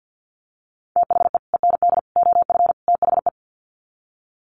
Morse code, heard as a keyed tone, about 50 wpm, 700 Hz. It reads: THE END OF THE
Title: Sets of 4 Words 50wpm